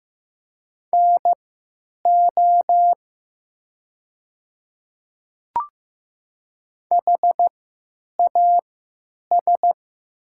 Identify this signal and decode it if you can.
Morse code, heard as a keyed tone, about 15 wpm, spaced as for 9 wpm, 700 Hz. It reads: NO HAS
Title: 3x Character Spacing - Top 100 Words - 15wpm